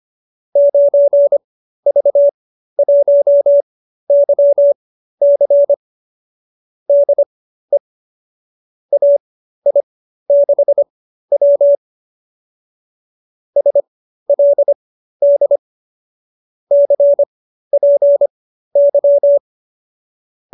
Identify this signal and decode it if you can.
Morse code, heard as a keyed tone, about 25 wpm, spaced as for 13 wpm, 590 Hz. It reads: 9V1YC DE AI6W SLD CPY